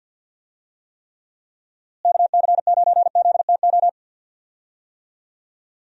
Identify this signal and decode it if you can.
Morse code, heard as a keyed tone, about 51 wpm, 700 Hz. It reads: KC9ZTO
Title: USA Call Signs - 6 Characters - Repeated 3x - 50wpm